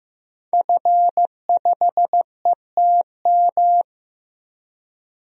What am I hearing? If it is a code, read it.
Morse code, heard as a keyed tone, about 15 wpm, 700 Hz. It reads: F5ETM
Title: DXCC-100 Call Signs - 5 Characters - Repeated 2x - 15wpm